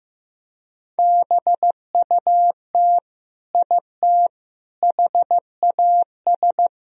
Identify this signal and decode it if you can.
Morse code, heard as a keyed tone, about 16 wpm, 700 Hz. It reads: BUT IT HAS
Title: Sentences from Top 100 Words and Abbreviations 15wpm